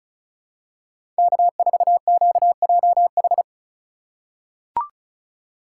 Morse code, heard as a keyed tone, about 36 wpm, 700 Hz. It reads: K4QJH
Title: USA Call Signs - 2x Character Spacing - 35wpm